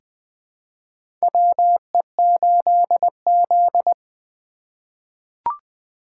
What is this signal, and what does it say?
Morse code, heard as a keyed tone, about 20 wpm, 700 Hz. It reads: WE8Z
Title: USA Call Signs - 4 Characters - 2x Character Spacing - 20wpm